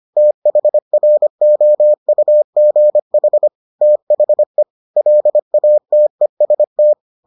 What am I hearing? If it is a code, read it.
Morse code, heard as a keyed tone, about 25 wpm, 600 Hz. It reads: THROUGH THE LATEST